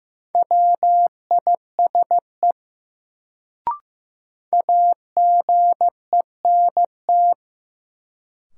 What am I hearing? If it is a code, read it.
Morse code, heard as a keyed tone, about 15 wpm, 700 Hz. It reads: WISE AGENT
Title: Sentences from Top 1600 Words - New Word Drill Alt - 15wpm